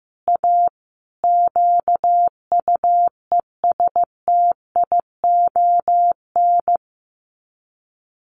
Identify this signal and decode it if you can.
Morse code, heard as a keyed tone, about 15 wpm, 700 Hz. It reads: A QUESTION
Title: Sets of 5 Words Encore2 15wpm